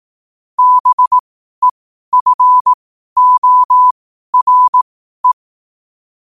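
Morse code, heard as a keyed tone, about 18 wpm, spaced as for 12 wpm, 1000 Hz. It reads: BEFORE
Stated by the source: Morse code quote practice at 12 words per minute.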